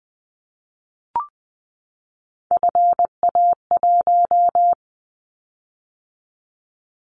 Morse code, heard as a keyed tone, about 20 wpm, 700 Hz. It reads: FA1
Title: Callsign Prefixes for Top 50 DXCC - 20wpm